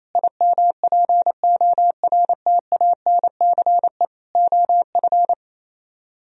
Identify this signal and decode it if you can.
Morse code, heard as a keyed tone, about 34 wpm, 700 Hz. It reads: IMPORTANCE OF
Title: Speed-Racing - Sets of 3 Words 28wpm